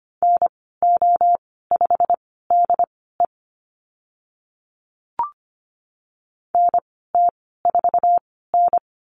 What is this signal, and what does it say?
Morse code, heard as a keyed tone, about 25 wpm, spaced as for 16 wpm, 690 Hz. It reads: NO5DE NT4N
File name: Lesson 15g - Number 5 - Call Signs 25wpm